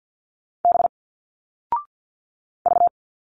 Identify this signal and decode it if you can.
Morse code, heard as a keyed tone, about 50 wpm, 700 Hz. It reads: B V
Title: Single Letters 50wpm